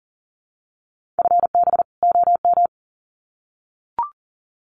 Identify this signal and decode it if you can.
Morse code, heard as a keyed tone, about 40 wpm, 700 Hz. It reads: FB OM
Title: Common QSO Phrases 40wpm